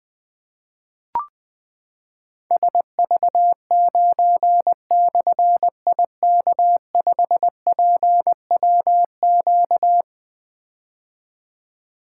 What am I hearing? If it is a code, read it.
Morse code, heard as a keyed tone, about 20 wpm, 700 Hz. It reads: SV9/IK5PWQ
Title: Difficult Call Signs 20wpm